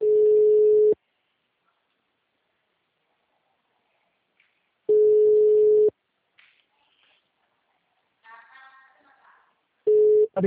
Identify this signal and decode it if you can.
Ringback tone (Europe, 425 Hz), with speech.
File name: out- -2019- - - .0